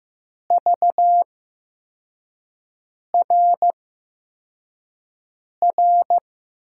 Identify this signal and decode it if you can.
Morse code, heard as a keyed tone, about 15 wpm, spaced as for 4 wpm, 700 Hz. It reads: VRR
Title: USA Call Signs - 6 Characters - 8x Character Spacing - 15wpm